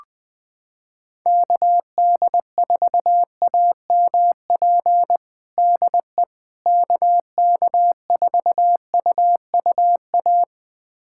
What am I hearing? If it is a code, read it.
Morse code, heard as a keyed tone, about 20 wpm, 700 Hz. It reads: KD4AMP DE KK4UUA
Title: USA Callsign DE Callsign 20wpm